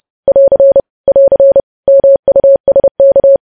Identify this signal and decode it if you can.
Morse code, heard as a keyed tone, about 30 wpm, 550 Hz. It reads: <AR> <AR> MUSK
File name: podcast / News headlines-1x-30x30